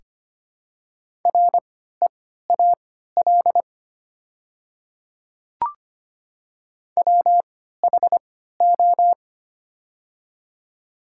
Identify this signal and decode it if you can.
Morse code, heard as a keyed tone, about 25 wpm, spaced as for 14 wpm, 700 Hz. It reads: REAL WHO
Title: 3x Character Spacing - Top 300 Words - 25wpm